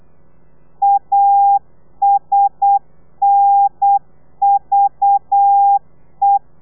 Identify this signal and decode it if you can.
Morse code, heard as a keyed tone, about 8 wpm, 800 Hz. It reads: ASNVE
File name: telegrafia / i2tfj / Lez4 35